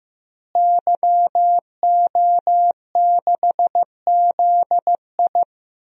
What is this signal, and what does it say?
Morse code, heard as a keyed tone, about 15 wpm, 700 Hz. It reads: YO6ZI
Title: DXCC-100 Call Signs - 5 Characters - Repeated 2x - 15wpm